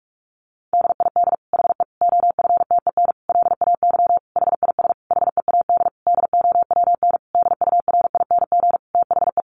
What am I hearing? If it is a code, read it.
Morse code, heard as a keyed tone, about 44 wpm, 700 Hz. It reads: DID HE OFTEN LAY HIS HEAD DOWN DURING THE
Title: Sentences from Top 1000 Words 45wpm